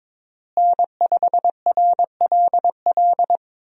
Morse code, heard as a keyed tone, about 21 wpm, 700 Hz. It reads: N5RLL
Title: USA Call Signs - 5 Characters - Repeated 2x - 22wpm